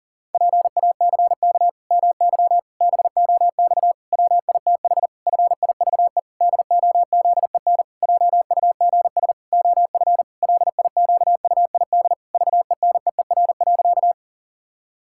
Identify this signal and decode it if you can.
Morse code, heard as a keyed tone, about 40 wpm, 700 Hz. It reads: PACK MY BOX WITH FIVE DOZEN JUGS OF LIQUID VENEER.
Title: Pangram Challenge v2 - 40wpm